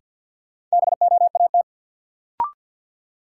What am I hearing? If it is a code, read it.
Morse code, heard as a keyed tone, about 51 wpm, 700 Hz. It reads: BOAT
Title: Sentences from Top 500 Words - New Word Drill - 50wpm